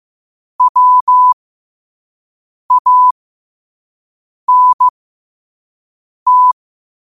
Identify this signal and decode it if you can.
Morse code, heard as a keyed tone, about 15 wpm, spaced as for 5 wpm, 1000 Hz. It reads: WANT
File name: qotdfiles / 5wpm